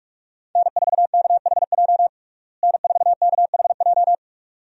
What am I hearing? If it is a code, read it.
Morse code, heard as a keyed tone, about 47 wpm, 700 Hz. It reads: N4KHJ N4KHJ
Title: USA Call Signs - 5 Characters - Repeated 2x - 45wpm